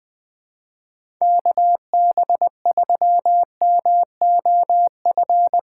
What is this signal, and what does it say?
Morse code, heard as a keyed tone, about 21 wpm, 700 Hz. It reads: KB3MOF
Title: USA Callsign DE Callsign 20wpm